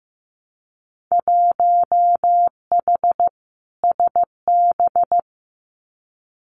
Morse code, heard as a keyed tone, about 15 wpm, 700 Hz. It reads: 1H SB
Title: Winter Field Day Exchanges 15wpm